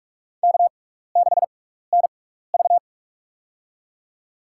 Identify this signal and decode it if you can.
Morse code, heard as a keyed tone, about 46 wpm, spaced as for 15 wpm, 700 Hz. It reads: K6NV
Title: USA Call Signs - 4 Characters - 6x Character Spacing - 45wpm